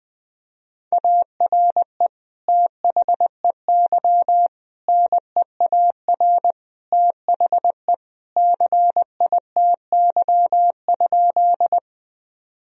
Morse code, heard as a keyed tone, about 21 wpm, 700 Hz. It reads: ARE THEY NEAR THE CITY?
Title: Sentences from Top 1000 Words - Review Part 1 - 20wpm